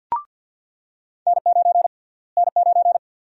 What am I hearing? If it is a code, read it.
Morse code, heard as a keyed tone, about 46 wpm, 700 Hz. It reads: N9 N9
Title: USA Call Sign Prefixes - Repeated 2x - 50wpm